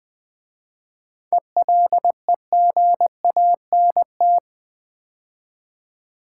Morse code, heard as a keyed tone, about 20 wpm, 700 Hz. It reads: ELEGANT
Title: 7 Letter Words - First 3 Letters Given - 20wpm